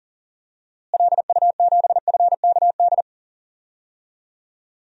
Morse code, heard as a keyed tone, about 37 wpm, 700 Hz. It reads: LU7FKD
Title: DXCC-100 Call Signs - 6 Characters - Repeated 2x - 40wpm